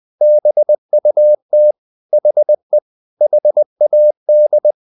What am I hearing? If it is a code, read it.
Morse code, heard as a keyed tone, about 20 wpm, 600 Hz. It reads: BUT HE HAD
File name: the old man and the sea part 2 20wpm